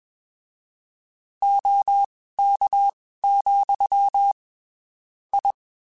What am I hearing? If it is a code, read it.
Morse code, heard as a keyed tone, about 21 wpm, spaced as for 15 wpm, 800 Hz. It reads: OK, I
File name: (spaced) / QSOs Spaced-mp3s / WA7QOX 2 @20 sp